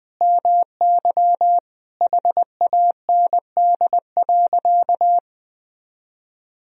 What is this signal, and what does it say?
Morse code, heard as a keyed tone, about 21 wpm, 700 Hz. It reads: MY HAND.
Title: Sentences from Top 1000 Words - Review Part 1 - 20wpm